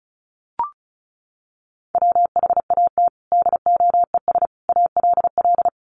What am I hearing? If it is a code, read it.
Morse code, heard as a keyed tone, about 34 wpm, 700 Hz. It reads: WHAT DOES ALL
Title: Sentences from Top 200 Words 35wpm